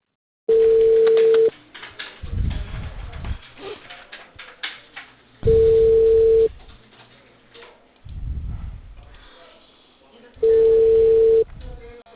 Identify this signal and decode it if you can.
Ringback tone (China, 450 Hz), with speech.